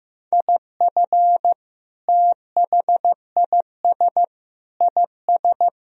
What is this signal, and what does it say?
Morse code, heard as a keyed tone, about 16 wpm, 700 Hz. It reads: IF THIS IS
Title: Sentences from Top 1000 Words - Review Part 1 - 15wpm